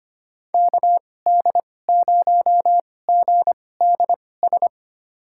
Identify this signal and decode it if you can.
Morse code, heard as a keyed tone, about 25 wpm, spaced as for 18 wpm, 700 Hz. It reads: KD0GDS
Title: USA Call Signs - 6 Characters - 2x Character Spacing - 25wpm